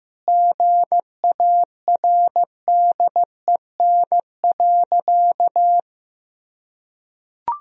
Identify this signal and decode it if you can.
Morse code, heard as a keyed tone, about 15 wpm, 700 Hz. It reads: GARDEN.
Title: Sentences from Top 1000 Words - Review Part 4 - 15wpm